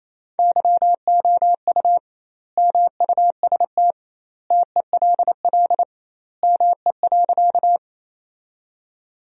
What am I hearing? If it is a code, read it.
Morse code, heard as a keyed tone, about 28 wpm, 700 Hz. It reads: YOU MUST TELL ME.
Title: 2x Word-Spacing - Sentences from Top 200 Words - 28wpm